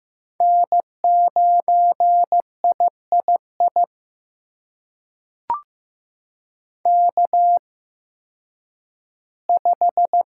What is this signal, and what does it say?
Morse code, heard as a keyed tone, about 15 wpm, spaced as for 10 wpm, 700 Hz. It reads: N9III K5
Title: USA Call Signs - 5 Characters - 8x Character Spacing - 15wpm